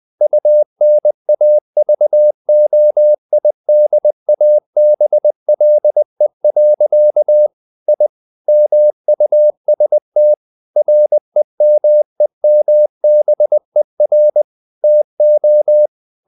Morse code, heard as a keyed tone, about 20 wpm, 600 Hz. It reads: UNAVOIDABLE. I MUST REMEMBER TO
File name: the old man and the sea / the old man and the sea part 1 20wpm